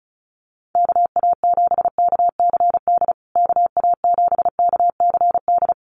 Morse code, heard as a keyed tone, about 34 wpm, 700 Hz. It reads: KA7KCD KA7KCD
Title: USA Call Signs Repeated 3x 35wpm